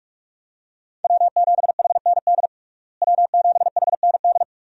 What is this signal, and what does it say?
Morse code, heard as a keyed tone, about 43 wpm, 700 Hz. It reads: W7HND W7HND
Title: USA Call Signs - 5 Characters - Repeated 2x - 45wpm